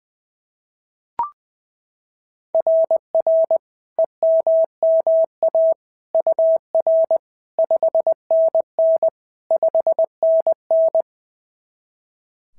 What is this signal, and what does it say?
Morse code, heard as a keyed tone, about 18 wpm, 660 Hz. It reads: RR EMMA UR 5NN 5NN
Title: SOTA QSO Practice 20wpm